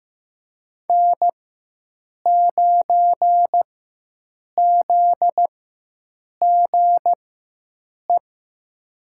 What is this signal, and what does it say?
Morse code, heard as a keyed tone, about 15 wpm, spaced as for 7 wpm, 700 Hz. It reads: N9ZGE
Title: USA Call Signs - 5 Characters - 4x Character Spacing - 15wpm